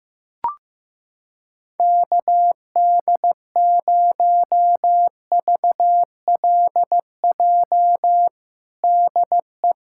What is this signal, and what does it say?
Morse code, heard as a keyed tone, about 15 wpm, 700 Hz. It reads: KD0VLJ DE
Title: USA Callsign DE Callsign - 6 Characters - 15wpm